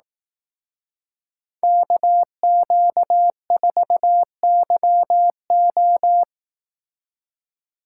Morse code, heard as a keyed tone, about 18 wpm, 700 Hz. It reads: KQ4YO
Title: Speed Racing - USA 2x2 Call Signs 15wpm